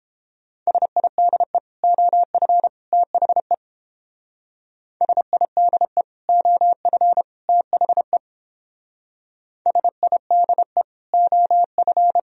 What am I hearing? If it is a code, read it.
Morse code, heard as a keyed tone, about 30 wpm, 700 Hz. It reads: SIDE OF THE SIDE OF THE SIDE OF